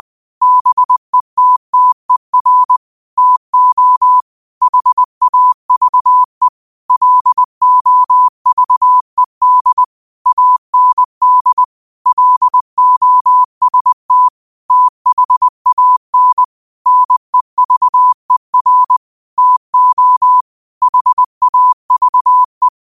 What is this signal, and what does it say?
Morse code, heard as a keyed tone, about 20 wpm, 1000 Hz. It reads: BETTER TO HAVE LOVED AND LOST THAN NEVER TO HAVE
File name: qotdfiles / 20wpm